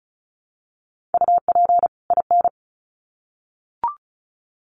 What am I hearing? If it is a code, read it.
Morse code, heard as a keyed tone, about 34 wpm, 700 Hz. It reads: UP IN
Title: Sets of 2 Words 35wpm